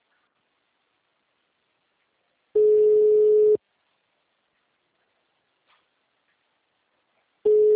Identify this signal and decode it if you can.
Ringback tone (Europe, 425 Hz).